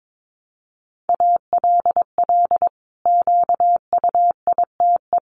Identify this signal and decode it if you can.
Morse code, heard as a keyed tone, about 22 wpm, 700 Hz. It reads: ALL QUITE
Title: Sentences from Top 700 Words 22wpm